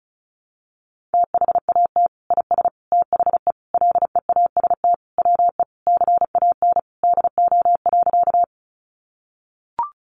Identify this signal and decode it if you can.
Morse code, heard as a keyed tone, about 34 wpm, 700 Hz. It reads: THAT IS THE LEAST WE CAN DO.
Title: Sentences from Top 700 Words 35wpm